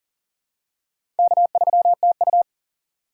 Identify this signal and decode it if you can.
Morse code, heard as a keyed tone, about 40 wpm, 700 Hz. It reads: K3TU
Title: USA Call Signs - 4 Characters - 6x Character Spacing - 40wpm